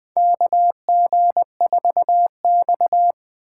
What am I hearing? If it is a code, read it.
Morse code, heard as a keyed tone, about 20 wpm, 700 Hz. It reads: KG4X
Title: USA Call Signs - 2x Character Spacing - 20wpm